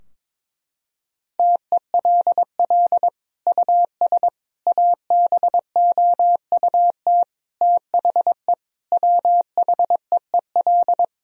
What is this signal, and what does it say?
Morse code, heard as a keyed tone, about 21 wpm, 700 Hz. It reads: TELL US ABOUT THE WHEEL
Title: Sentences from Top 1000 Words - Review Part 3 - 22wpm